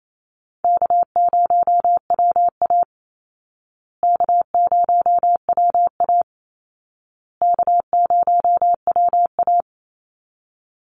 Morse code, heard as a keyed tone, about 28 wpm, 700 Hz. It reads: K0WA K0WA K0WA
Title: Easy Call Signs Repeated 3x - 28wpm